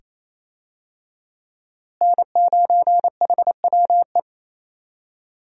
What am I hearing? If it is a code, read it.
Morse code, heard as a keyed tone, about 28 wpm, 700 Hz. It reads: N9HWE
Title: USA Call Signs - 5 Characters - Repeated 2x - 28wpm